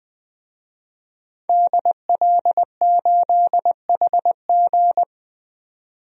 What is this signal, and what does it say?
Morse code, heard as a keyed tone, about 20 wpm, 700 Hz. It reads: DL8HG
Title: DXCC-100 Call Signs - 5 Characters - 20wpm